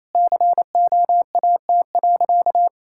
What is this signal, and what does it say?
Morse code, heard as a keyed tone, about 28 wpm, 700 Hz. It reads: COAT.
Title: Sentences from Top 1600 Words 28wpm